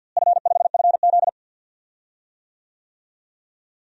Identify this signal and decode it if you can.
Morse code, heard as a keyed tone, about 46 wpm, 700 Hz. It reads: F5FZ
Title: DXCC-100 Call Signs - 4 Characters - Repeated 2x - 50wpm